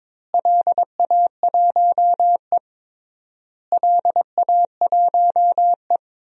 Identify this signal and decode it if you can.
Morse code, heard as a keyed tone, about 22 wpm, 700 Hz. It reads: LA1E LA1E
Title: DXCC-100 Call Signs - 4 Characters - Repeated 3x - 22wpm